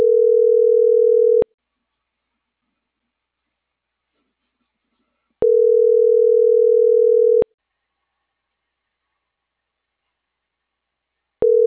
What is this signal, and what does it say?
Ringback tone (North America), 440 + 480 Hz.